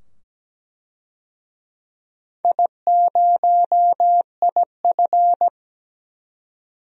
Morse code, heard as a keyed tone, about 17 wpm, 700 Hz. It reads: I0IF
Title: DXCC-100 Call Signs - 4 Characters - 17wpm